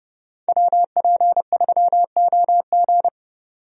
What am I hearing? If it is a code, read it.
Morse code, heard as a keyed tone, about 32 wpm, 700 Hz. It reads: WP3OG
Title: USA Call Signs - 5 Characters - 30wpm